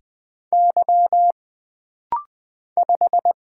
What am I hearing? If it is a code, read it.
Morse code, heard as a keyed tone, about 20 wpm, 700 Hz. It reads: Y 5
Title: Single Letter-Number - Rapid-Fire 20wpm